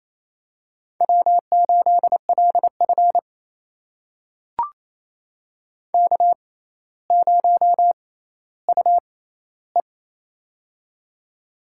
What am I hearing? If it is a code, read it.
Morse code, heard as a keyed tone, about 28 wpm, spaced as for 14 wpm, 700 Hz. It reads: W8LF K0UE
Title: USA Call Signs - 4 Characters - 6x Character Spacing - 28wpm